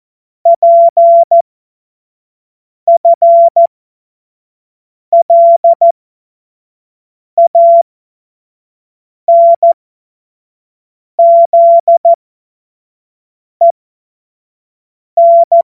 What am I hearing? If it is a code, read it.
Morse code, heard as a keyed tone, about 14 wpm, spaced as for 5 wpm, 680 Hz. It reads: PFLANZEN
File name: Morsekurs / Witze / Witz 14.November 2023 25 F70